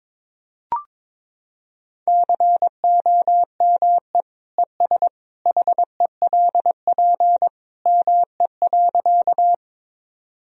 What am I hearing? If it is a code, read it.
Morse code, heard as a keyed tone, about 21 wpm, 700 Hz. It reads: COME ES HELP ME.
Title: Sentences from Top 200 Words and Abbreviations 22wpm